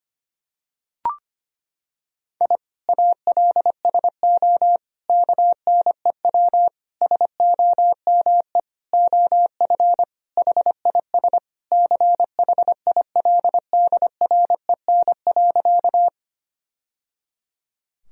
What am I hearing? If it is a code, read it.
Morse code, heard as a keyed tone, about 25 wpm, 700 Hz. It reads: I ALSO KNEW SOME OF HIS CHILDREN.
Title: Sentences from Top 1000 Words - Review Part 1 - 25wpm